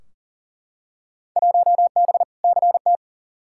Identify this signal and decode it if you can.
Morse code, heard as a keyed tone, about 40 wpm, 700 Hz. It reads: 1B CT
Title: Field Day Exchanges Encore - 40wpm